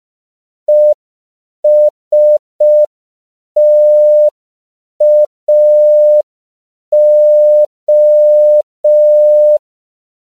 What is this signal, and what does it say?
Morse code, heard as a keyed tone, about 5 wpm, 600 Hz. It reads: ESTAO
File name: CW / 5PPM